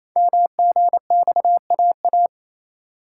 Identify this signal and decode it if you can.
Morse code, heard as a keyed tone, about 27 wpm, 700 Hz. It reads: MGXAA
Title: Code Groups - 5 Letters - 28wpm